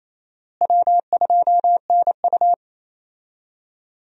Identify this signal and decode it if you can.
Morse code, heard as a keyed tone, about 28 wpm, 700 Hz. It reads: W2NU
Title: USA Call Signs - 4 Characters - Repeated 3x - 28wpm